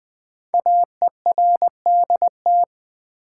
Morse code, heard as a keyed tone, about 20 wpm, 700 Hz. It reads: AERDT
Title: Code Groups - 5 Letters - 20wpm